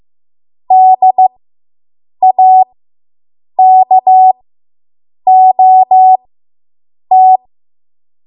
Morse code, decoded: DAKOT